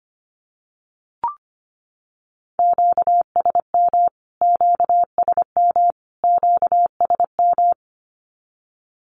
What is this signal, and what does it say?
Morse code, heard as a keyed tone, about 25 wpm, 700 Hz. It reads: QSM QSM QSM
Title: Q-Codes Repeated 3x 25wpm